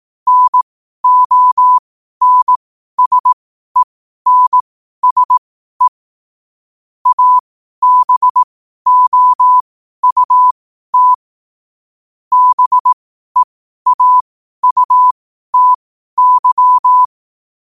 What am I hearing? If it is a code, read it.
Morse code, heard as a keyed tone, about 18 wpm, spaced as for 12 wpm, 1000 Hz. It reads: NONSENSE ABOUT BEAUTY
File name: qotdfiles / 12wpm